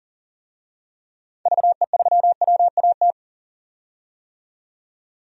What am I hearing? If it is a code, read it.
Morse code, heard as a keyed tone, about 40 wpm, 700 Hz. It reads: VE3WAT
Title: DXCC-100 Call Signs - 6 Characters - 40wpm